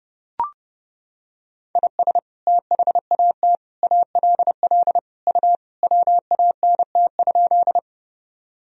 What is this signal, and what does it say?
Morse code, heard as a keyed tone, about 30 wpm, 700 Hz. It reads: IS THAT ALL U WANT?
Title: Sentences from Top 200 Words and Abbreviations 30wpm